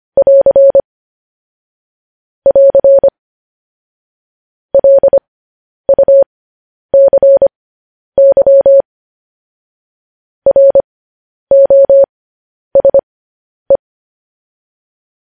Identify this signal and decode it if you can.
Morse code, heard as a keyed tone, about 25 wpm, spaced as for 10 wpm, 550 Hz. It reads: <AR> <AR> LUCY ROSE
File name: podcast / News headlines-1x-25x10